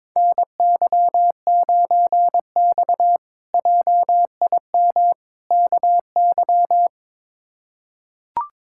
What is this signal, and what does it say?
Morse code, heard as a keyed tone, about 22 wpm, 700 Hz. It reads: NY9X JIM KY
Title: K1USN SST QSO Practice - 22wpm